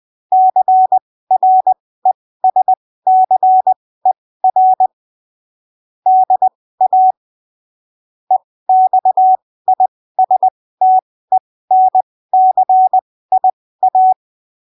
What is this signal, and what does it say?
Morse code, heard as a keyed tone, about 20 wpm, spaced as for 15 wpm, 750 Hz. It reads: CRESCER DA EXISTENCIA
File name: arquivos / texto / texto 29 20ppm